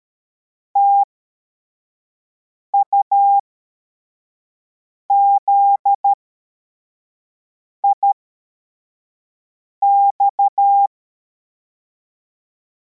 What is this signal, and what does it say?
Morse code, heard as a keyed tone, about 13 wpm, spaced as for 4 wpm, 800 Hz. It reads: TUZIX
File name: formation / cw / Text53